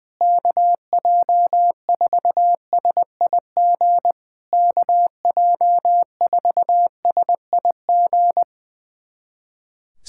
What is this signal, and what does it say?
Morse code, heard as a keyed tone, about 20 wpm, 700 Hz. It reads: KJ4SIG KJ4SIG
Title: SOTA QSO Practice 20wpm